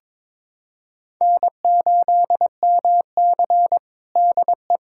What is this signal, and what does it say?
Morse code, heard as a keyed tone, about 22 wpm, 700 Hz. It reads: N8MC DE
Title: USA Callsign DE Callsign 22wpm